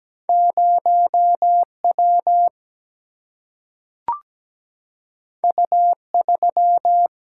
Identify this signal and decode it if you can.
Morse code, heard as a keyed tone, about 17 wpm, 700 Hz. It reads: 0 W U 3
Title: Single-Letter Single-Number 17wpm